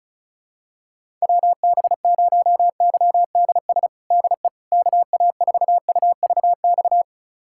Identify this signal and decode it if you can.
Morse code, heard as a keyed tone, about 34 wpm, 700 Hz. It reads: WB0YDS DE KA4UVX
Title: USA Callsign DE Callsign - 6 Characters - 35wpm